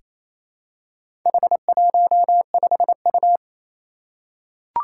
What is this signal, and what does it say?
Morse code, heard as a keyed tone, about 27 wpm, 700 Hz. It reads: H15U
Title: Four Challenging Letter-Number Combinations 28wpm